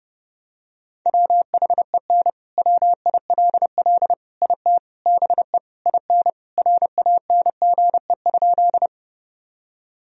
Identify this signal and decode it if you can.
Morse code, heard as a keyed tone, about 31 wpm, 700 Hz. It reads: WHEN WILL IT BE IN RANGE?
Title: Sentences from Top 1000 Words - Review Part 4 - 30wpm